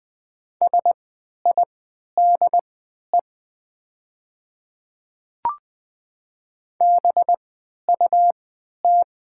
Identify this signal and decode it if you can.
Morse code, heard as a keyed tone, about 20 wpm, spaced as for 11 wpm, 700 Hz. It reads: SIDE BUT
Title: 3x Character Spacing - Top 100 Words - 20wpm